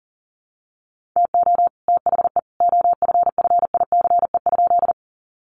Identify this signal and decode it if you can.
Morse code, heard as a keyed tone, about 40 wpm, 700 Hz. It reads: TO THE OFFICE?
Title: Sentences from Top 1000 Words 40wpm